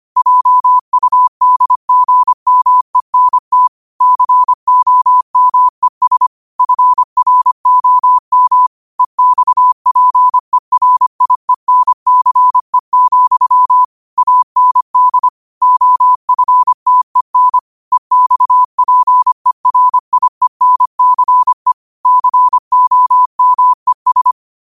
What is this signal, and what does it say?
Morse code, heard as a keyed tone, about 25 wpm, 1000 Hz. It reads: JUDGMENT COMES FROM EXPERIENCE, AND OFTEN EXPERIENCE COMES
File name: 25wpm